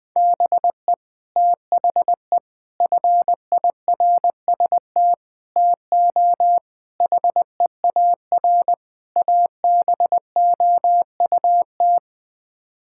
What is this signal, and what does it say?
Morse code, heard as a keyed tone, about 19 wpm, 700 Hz. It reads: BE THE FIRST TO HEAR ABOUT
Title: Sets of 6 Words - 20wpm